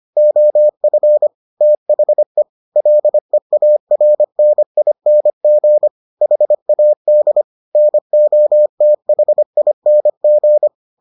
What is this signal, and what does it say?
Morse code, heard as a keyed tone, about 25 wpm, 600 Hz. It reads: OF THE LEARNING HAD NOTHING